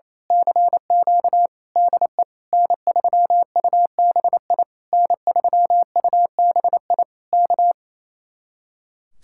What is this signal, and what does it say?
Morse code, heard as a keyed tone, about 26 wpm, 700 Hz. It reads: CQ DE N3UBI N3UBI K
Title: Speed-Racing Common QSO Phrases - 28wpm